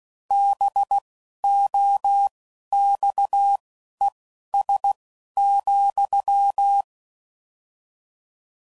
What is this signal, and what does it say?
Morse code, heard as a keyed tone, about 16 wpm, spaced as for 11 wpm, 800 Hz. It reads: BOXES,